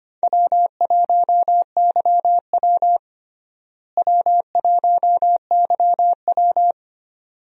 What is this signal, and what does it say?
Morse code, heard as a keyed tone, about 25 wpm, 700 Hz. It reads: W1YW W1YW
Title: USA Call Signs - 4 Characters - Repeated 2x - 25wpm